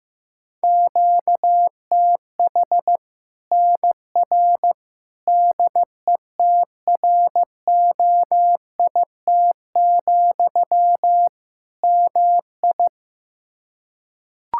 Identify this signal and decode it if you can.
Morse code, heard as a keyed tone, about 15 wpm, 700 Hz. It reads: QTH NR DETROIT, MI
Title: Practice QSO - Exchange 1 - Beginning - 15wpm